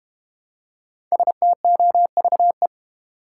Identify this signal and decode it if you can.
Morse code, heard as a keyed tone, about 31 wpm, 700 Hz. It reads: STOVE
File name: Easy Words - Speed-Racing with Hints - 25wpm